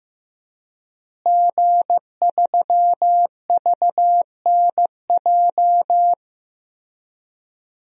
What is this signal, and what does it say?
Morse code, heard as a keyed tone, about 15 wpm, 700 Hz. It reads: G3VNJ
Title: DXCC-100 Call Signs - 5 Characters - Repeated 3x - 15wpm